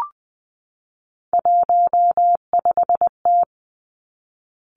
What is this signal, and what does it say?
Morse code, heard as a keyed tone, about 20 wpm, 700 Hz. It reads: 15T
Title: Three Challenging Letter-Number Combinations 20wpm